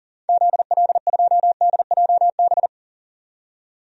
Morse code, heard as a keyed tone, about 40 wpm, 700 Hz. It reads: ZL2DJB
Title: DXCC-100 Call Signs - 6 Characters - 40wpm